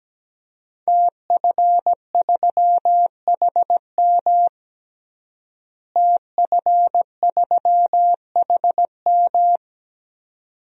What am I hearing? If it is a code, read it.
Morse code, heard as a keyed tone, about 17 wpm, 700 Hz. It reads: TF3HM TF3HM
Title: DXCC-100 Call Signs - 5 Characters - Repeated 2x - 17wpm